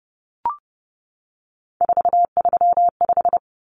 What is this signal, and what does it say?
Morse code, heard as a keyed tone, about 30 wpm, 700 Hz. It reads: 435
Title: Common Cut Numbers - 3-Digit 30wpm